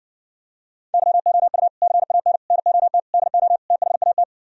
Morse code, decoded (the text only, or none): YOU CAN NOT DO THAT